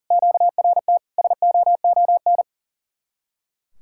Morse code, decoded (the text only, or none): QRT SOON